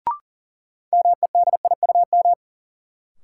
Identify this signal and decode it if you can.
Morse code, heard as a keyed tone, about 40 wpm, 700 Hz. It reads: MEDIUM
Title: Sentences from Top 900 Words - New Word Drill - 40wpm